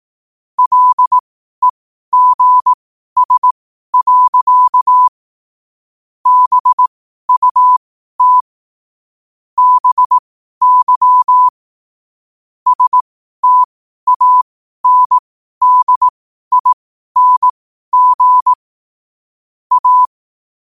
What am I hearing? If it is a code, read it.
Morse code, heard as a keyed tone, about 18 wpm, spaced as for 12 wpm, 1000 Hz. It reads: LEGS. BUT BY STANDING A
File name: qotdfiles / 12wpm